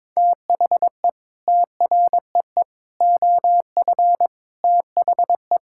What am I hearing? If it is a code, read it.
Morse code, heard as a keyed tone, about 22 wpm, 700 Hz. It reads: THE TREE OF THE
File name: Sets of 9 Words - 22wpm